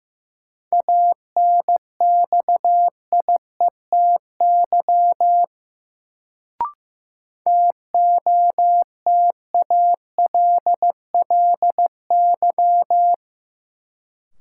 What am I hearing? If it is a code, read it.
Morse code, heard as a keyed tone, about 15 wpm, 700 Hz. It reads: ANXIETY TOTALLY
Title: Sentences from Top 1600 Words - New Word Drill Alt - 15wpm